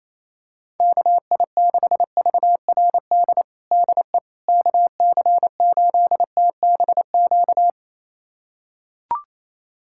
Morse code, heard as a keyed tone, about 27 wpm, 700 Hz. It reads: KI6VRD DE KC8TBQ
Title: USA Callsign DE Callsign 28wpm